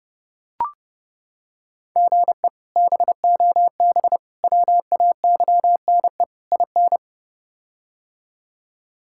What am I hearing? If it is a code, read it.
Morse code, heard as a keyed tone, about 30 wpm, 700 Hz. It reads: GE BOB WAYNE IN
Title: K1USN SST QSO Practice - 30wpm